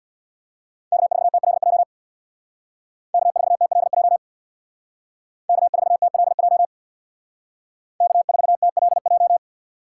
Morse code, heard as a keyed tone, about 64 wpm, 700 Hz. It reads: K4TLJ K4TLJ K4TLJ K4TLJ